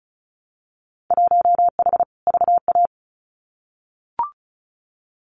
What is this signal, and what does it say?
Morse code, heard as a keyed tone, about 34 wpm, 700 Hz. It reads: 1H VA
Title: Winter Field Day Exchanges 35wpm